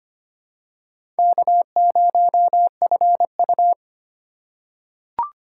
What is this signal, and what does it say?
Morse code, heard as a keyed tone, about 25 wpm, 700 Hz. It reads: K0FU
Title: USA Call Signs - 4 Characters - 3x Character Spacing - 25wpm